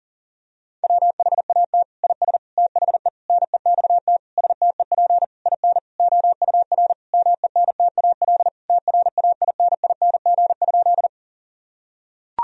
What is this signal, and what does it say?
Morse code, heard as a keyed tone, about 40 wpm, 700 Hz. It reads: WHAT IS THE NEXT STEP IN OUR MENTAL TRAINING?
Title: Sentences from Top 1000 Words - Review Part 4 - 40wpm